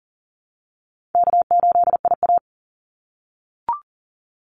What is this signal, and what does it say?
Morse code, heard as a keyed tone, about 40 wpm, 700 Hz. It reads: K8IA
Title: Easy Call Signs - 40wpm